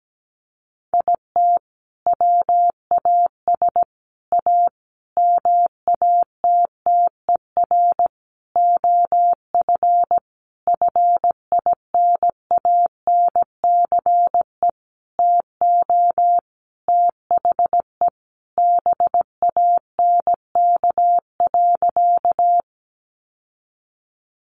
Morse code, heard as a keyed tone, about 17 wpm, 700 Hz. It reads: IT WAS A MATTER OF FINANCE TO THE BANK.
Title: Sentences from Top 1500 Words 17wpm